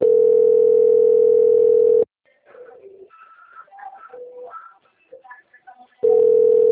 Ringback tone (North America), 440 + 480 Hz.
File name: out- - - - - .27199